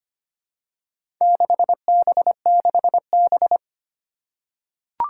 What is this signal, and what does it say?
Morse code, heard as a keyed tone, about 25 wpm, 700 Hz. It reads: 6B6B
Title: Four Challenging Letter-Number Combinations 25wpm